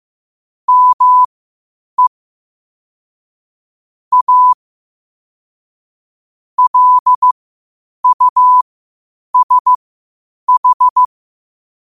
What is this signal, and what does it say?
Morse code, heard as a keyed tone, about 15 wpm, spaced as for 8 wpm, 1000 Hz. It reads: ME A LUSH
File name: qotdfiles / 8wpm